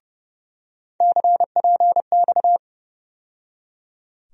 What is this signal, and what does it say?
Morse code, heard as a keyed tone, about 30 wpm, 700 Hz. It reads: CPX
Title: USA Call Sign Postfixes - 3 Letters 30wpm